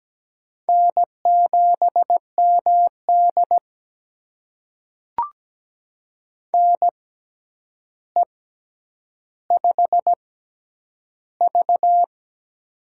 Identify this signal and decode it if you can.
Morse code, heard as a keyed tone, about 17 wpm, spaced as for 9 wpm, 700 Hz. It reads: N7MD NE5V
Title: USA Call Signs - 4 Characters - 6x Character Spacing - 17wpm